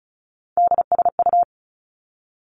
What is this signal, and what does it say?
Morse code, heard as a keyed tone, about 36 wpm, 700 Hz. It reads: DSU
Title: Three Challenging Letter-Number Combinations 35wpm